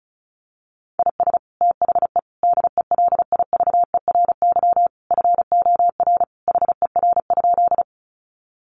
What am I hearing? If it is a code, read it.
Morse code, heard as a keyed tone, about 34 wpm, 700 Hz. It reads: IS THE DELIVERY FOR HER?
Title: Sentences from Top 1500 Words 35wpm